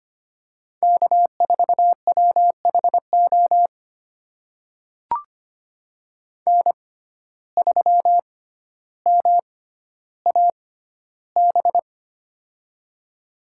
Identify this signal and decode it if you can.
Morse code, heard as a keyed tone, about 25 wpm, spaced as for 13 wpm, 700 Hz. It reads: K4WHO N3MAB
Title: USA Call Signs - 5 Characters - 6x Character Spacing - 25wpm